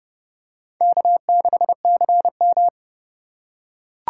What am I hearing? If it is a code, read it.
Morse code, heard as a keyed tone, about 31 wpm, 700 Hz. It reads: K6CM
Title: USA Call Signs - 4x Character Spacing - 30wpm